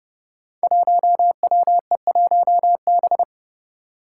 Morse code, heard as a keyed tone, about 30 wpm, 700 Hz. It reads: 1WE1B
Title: Five Challenging Letter-Number Combinations 30wpm